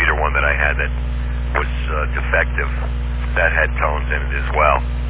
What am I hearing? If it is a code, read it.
Emergency Alert System (SAME) tones, heard as harsh buzzing bursts, with speech. It is heard from 0.5 to 4.4 s.